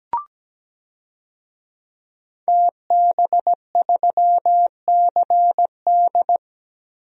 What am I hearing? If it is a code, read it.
Morse code, heard as a keyed tone, about 17 wpm, 700 Hz. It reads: TB3CD
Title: DXCC-100 Call Signs - 5 Characters - 17wpm